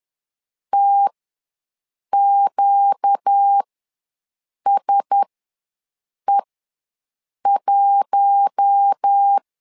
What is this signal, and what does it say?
Morse code, heard as a keyed tone, about 10 wpm, spaced as for 6 wpm, 800 Hz. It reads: TQSE1
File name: LEZIONI / 4 Q1S ws 10wpm 2208